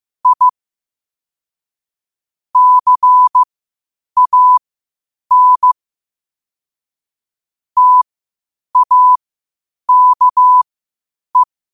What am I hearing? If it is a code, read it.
Morse code, heard as a keyed tone, about 15 wpm, spaced as for 8 wpm, 1000 Hz. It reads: I CAN TAKE